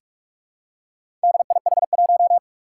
Morse code, heard as a keyed tone, about 46 wpm, 700 Hz. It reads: DIH1
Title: Four Challenging Letter-Number Combinations 45wpm